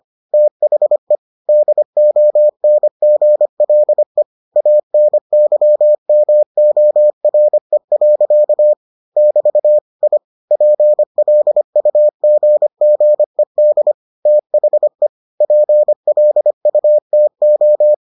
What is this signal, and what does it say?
Morse code, heard as a keyed tone, about 25 wpm, 600 Hz. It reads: THE DONGLE ANYMORE. <BT> I PLUGGED THE PLUTO